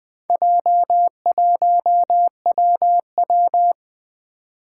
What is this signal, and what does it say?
Morse code, heard as a keyed tone, about 20 wpm, 700 Hz. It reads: J1WW
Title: Four Challenging Letter-Number Combinations 20wpm